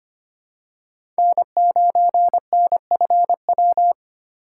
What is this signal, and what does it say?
Morse code, heard as a keyed tone, about 25 wpm, 700 Hz. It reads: N9NFW
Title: USA Call Signs - 2x Character Spacing - 25wpm